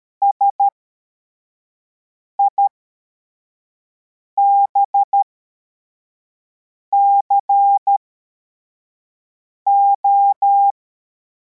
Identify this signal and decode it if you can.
Morse code, heard as a keyed tone, about 13 wpm, spaced as for 4 wpm, 800 Hz. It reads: SIBCO
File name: formation / cw / Text62